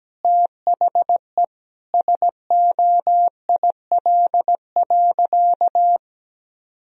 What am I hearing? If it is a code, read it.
Morse code, heard as a keyed tone, about 17 wpm, 700 Hz. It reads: THE SOIL.
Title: Sentences from Top 1000 Words - Review Part 4 - 17wpm